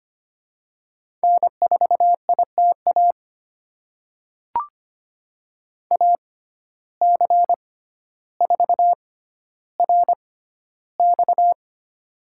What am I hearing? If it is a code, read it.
Morse code, heard as a keyed tone, about 25 wpm, spaced as for 13 wpm, 700 Hz. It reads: N4ITA AC4RX
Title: USA Call Signs - 5 Characters - 6x Character Spacing - 25wpm